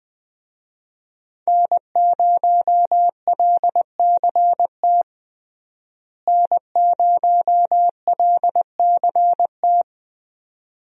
Morse code, heard as a keyed tone, about 20 wpm, 700 Hz. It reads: N0LCT N0LCT
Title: USA Call Signs - 5 Characters - Repeated 2x - 20wpm